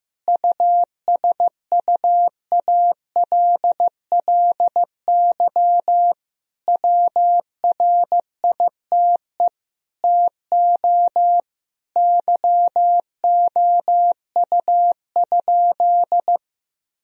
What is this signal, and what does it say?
Morse code, heard as a keyed tone, about 15 wpm, 700 Hz. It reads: USUALLY WRITE TO YOU?
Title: Sentences from Top 1000 Words - Review Part 3 - 15wpm